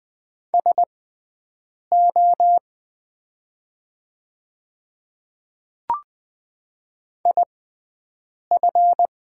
Morse code, heard as a keyed tone, about 20 wpm, spaced as for 7 wpm, 700 Hz. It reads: SO IF
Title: 6x Character Spacing - Top 200 Words - 20wpm